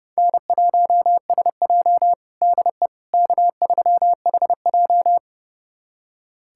Morse code, heard as a keyed tone, about 30 wpm, 700 Hz. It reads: N1SJ DE K3HJ
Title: USA Callsign DE Callsign - 4 Characters - 30wpm